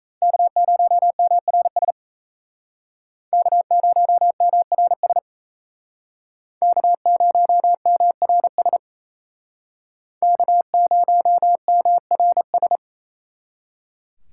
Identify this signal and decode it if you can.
Morse code, heard as a keyed tone, about 36 wpm, 700 Hz. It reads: K0MRS K0MRS K0MRS K0MRS